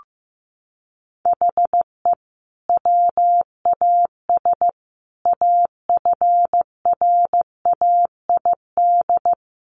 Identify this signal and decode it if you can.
Morse code, heard as a keyed tone, about 15 wpm, 700 Hz. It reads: HE WAS AFRAID
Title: Sentences from Top 1500 Words 15wpm